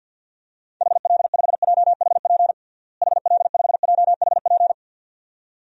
Morse code, heard as a keyed tone, about 53 wpm, 700 Hz. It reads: HL5JHP HL5JHP
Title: DXCC-100 Call Signs - 6 Characters - Repeated 2x - 50wpm